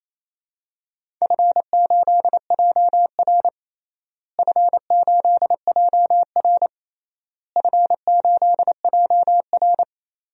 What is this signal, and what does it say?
Morse code, heard as a keyed tone, about 28 wpm, 700 Hz. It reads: F8JR F8JR F8JR
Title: DXCC-100 Call Signs - 4 Characters - Repeated 3x - 28wpm